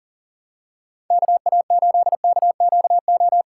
Morse code, heard as a keyed tone, about 38 wpm, 700 Hz. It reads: KA8KQO
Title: USA Call Signs - 6 Characters - 6x Character Spacing - 40wpm